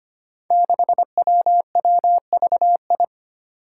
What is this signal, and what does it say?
Morse code, heard as a keyed tone, about 25 wpm, 700 Hz. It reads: 6WWVI
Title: Five Challenging Letter-Number Combinations 25wpm